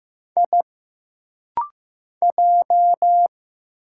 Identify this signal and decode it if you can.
Morse code, heard as a keyed tone, about 15 wpm, 700 Hz. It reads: I J
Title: Single Letters - Rapid-Fire 15wpm